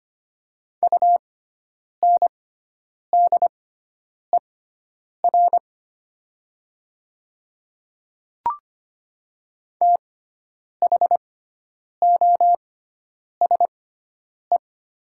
Morse code, heard as a keyed tone, about 25 wpm, spaced as for 9 wpm, 700 Hz. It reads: UNDER THOSE
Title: 6x Character Spacing - Top 300 Words - 25wpm